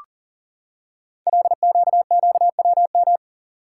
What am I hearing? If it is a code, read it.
Morse code, heard as a keyed tone, about 40 wpm, 700 Hz. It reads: LQQWM
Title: Code Groups - 5 Letters - 40wpm